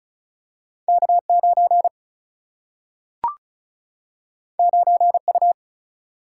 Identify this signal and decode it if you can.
Morse code, heard as a keyed tone, about 36 wpm, 700 Hz. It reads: K9 9U
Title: Single-Letter Single-Number 35wpm